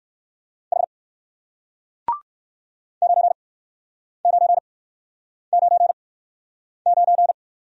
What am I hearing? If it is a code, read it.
Morse code, heard as a keyed tone, about 56 wpm, spaced as for 9 wpm, 700 Hz. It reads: H 9999